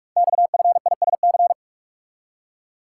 Morse code, heard as a keyed tone, about 44 wpm, 700 Hz. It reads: XFISC
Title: Code Groups - 5 Letters - 45wpm